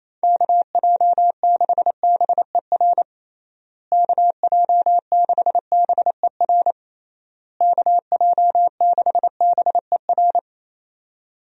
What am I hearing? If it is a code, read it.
Morse code, heard as a keyed tone, about 28 wpm, 700 Hz. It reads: KJ6BER KJ6BER KJ6BER
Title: USA Call Signs - 6 Characters - Repeated 3x - 28wpm